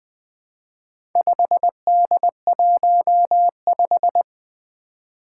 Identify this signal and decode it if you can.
Morse code, heard as a keyed tone, about 20 wpm, 700 Hz. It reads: 5D15
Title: Four Challenging Letter-Number Combinations 20wpm